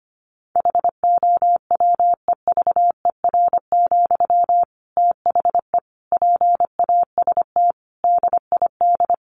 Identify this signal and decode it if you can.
Morse code, heard as a keyed tone, about 25 wpm, 700 Hz. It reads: HOWEVER, THE PAST DID
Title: Sentences from Top 1400 Words 25wpm